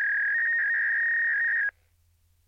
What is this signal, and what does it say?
Fax machine handshake.
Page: fax-dialling-fax-machine